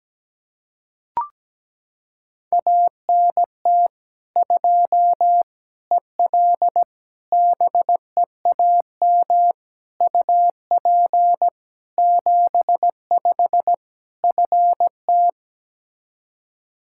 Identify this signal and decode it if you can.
Morse code, heard as a keyed tone, about 17 wpm, 700 Hz. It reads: ANT 2 EL BEAM UP 75 FT
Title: Practice QSO - Exchange 2 - Info - 17wpm